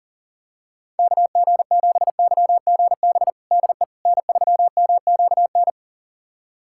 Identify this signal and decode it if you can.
Morse code, heard as a keyed tone, about 37 wpm, 700 Hz. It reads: KG7YGB DE N3MQN
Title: USA Callsign DE Callsign 40wpm